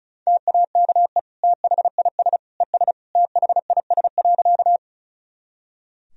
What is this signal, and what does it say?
Morse code, heard as a keyed tone, about 36 wpm, 700 Hz. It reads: TAKE THIS ES THIS.
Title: Sentences from Top 200 Words and Abbreviations 35wpm